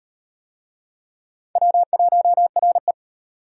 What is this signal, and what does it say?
Morse code, heard as a keyed tone, about 34 wpm, 700 Hz. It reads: W1RE